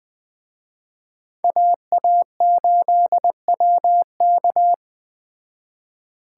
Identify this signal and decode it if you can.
Morse code, heard as a keyed tone, about 20 wpm, 700 Hz. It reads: AA8WK